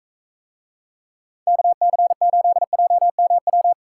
Morse code, heard as a keyed tone, about 42 wpm, 700 Hz. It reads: KC8JMW